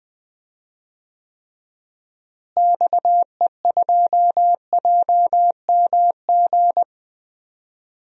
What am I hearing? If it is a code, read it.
Morse code, heard as a keyed tone, about 20 wpm, 700 Hz. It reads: XE2JMG
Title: DXCC-100 Call Signs - 6 Characters - Repeated 2x - 20wpm